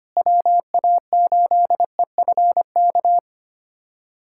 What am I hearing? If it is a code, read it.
Morse code, heard as a keyed tone, about 25 wpm, 700 Hz. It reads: WA8EFK
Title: USA Callsign DE Callsign 25wpm